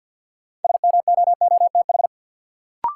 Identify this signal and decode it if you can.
Morse code, heard as a keyed tone, about 53 wpm, 700 Hz. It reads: SMOOTH
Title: Sentences from Top 1400 Words - New Word Drill Alt - 50wpm